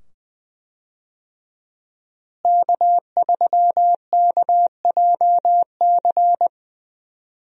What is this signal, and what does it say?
Morse code, heard as a keyed tone, about 20 wpm, 700 Hz. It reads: K3KJC
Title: USA Call Signs - 5 Characters - Repeated 2x - 20wpm